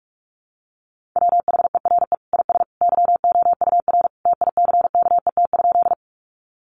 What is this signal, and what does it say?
Morse code, heard as a keyed tone, about 44 wpm, 700 Hz. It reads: WHERE IS YOUR TICKET?
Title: Sentences from Top 1400 Words 45wpm